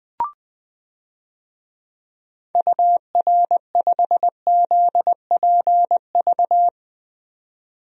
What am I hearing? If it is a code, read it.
Morse code, heard as a keyed tone, about 20 wpm, 700 Hz. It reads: UR5ZPV
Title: DXCC-100 Call Signs - 6 Characters - 20wpm